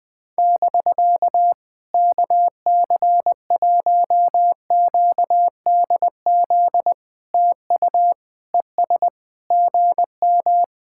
Morse code, heard as a keyed tone, about 20 wpm, 700 Hz. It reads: <BK> KC1QDZ TU ES GM
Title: POTA QSO Practice - 20wpm